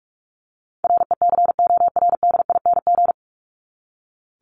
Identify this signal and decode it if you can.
Morse code, heard as a keyed tone, about 44 wpm, 700 Hz. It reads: RECORDING
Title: Top 1501-1750 Words 45wpm